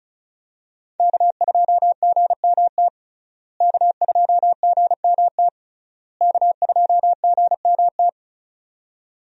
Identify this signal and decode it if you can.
Morse code, heard as a keyed tone, about 35 wpm, 700 Hz. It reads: K2GMT K2GMT K2GMT
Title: USA Call Signs - 5 Characters - Repeated 3x - 35wpm